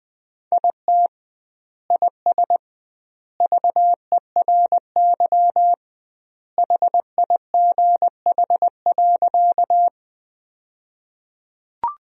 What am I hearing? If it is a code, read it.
Morse code, heard as a keyed tone, about 20 wpm, 700 Hz. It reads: IT IS VERY HIGH.
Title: 2x Word-Spacing - Sentences from Top 200 Words - 20wpm